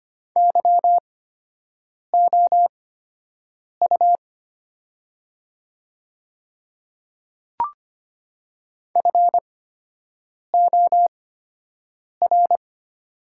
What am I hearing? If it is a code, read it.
Morse code, heard as a keyed tone, about 25 wpm, spaced as for 7 wpm, 700 Hz. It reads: YOU FOR
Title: 8x Character Spacing - Top 300 Words - 25wpm